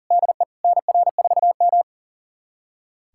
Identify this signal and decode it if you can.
Morse code, heard as a keyed tone, about 37 wpm, 700 Hz. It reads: DE NR4M
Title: USA Callsign DE Callsign - 4 Characters - 40wpm